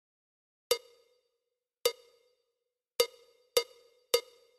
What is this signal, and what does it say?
There is no code.